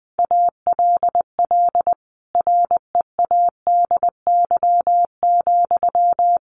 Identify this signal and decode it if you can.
Morse code, heard as a keyed tone, about 20 wpm, 700 Hz. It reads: ALL READY,
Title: Sentences from Top 1300 Words 20wpm